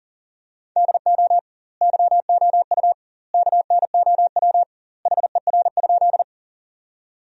Morse code, heard as a keyed tone, about 40 wpm, 700 Hz. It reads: DO YOU KNOW HER?
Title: 2x Word-Spacing - Sentences from Top 100 Words - 40wpm